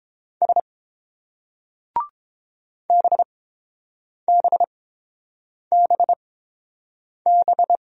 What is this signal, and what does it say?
Morse code, heard as a keyed tone, about 28 wpm, spaced as for 7 wpm, 700 Hz. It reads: S BBBB